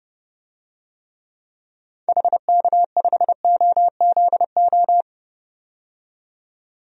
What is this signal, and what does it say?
Morse code, heard as a keyed tone, about 30 wpm, 690 Hz. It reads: HK5OZO
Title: DXCC-100 Call Signs - 6 Characters - 30wpm